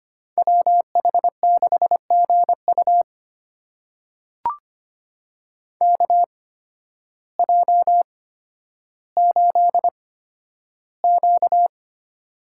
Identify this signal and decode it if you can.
Morse code, heard as a keyed tone, about 25 wpm, spaced as for 12 wpm, 700 Hz. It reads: WH6GU KJ8Q
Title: USA Call Signs - 8x Character Spacing - 25wpm